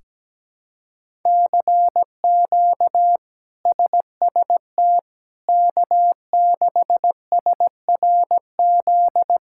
Morse code, heard as a keyed tone, about 17 wpm, 700 Hz. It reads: CQ SST K6SRZ
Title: K1USN SST QSO Practice - 17wpm